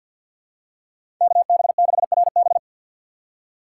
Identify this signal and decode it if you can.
Morse code, heard as a keyed tone, about 53 wpm, 700 Hz. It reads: KB6RB
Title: USA Call Signs - 5 Characters - 4x Character Spacing - 50wpm